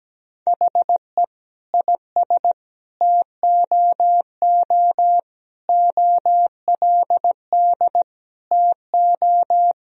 Morse code, heard as a keyed tone, about 17 wpm, 700 Hz. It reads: HE IS TOO OLD TO
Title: Sentences from Top 1000 Words - Review Part 4 - 17wpm